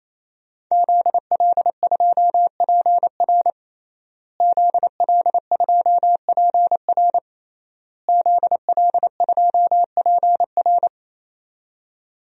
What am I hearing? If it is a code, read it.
Morse code, heard as a keyed tone, about 28 wpm, 700 Hz. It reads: ZL2PR ZL2PR ZL2PR
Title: DXCC-100 Call Signs - 5 Characters - Repeated 3x - 28wpm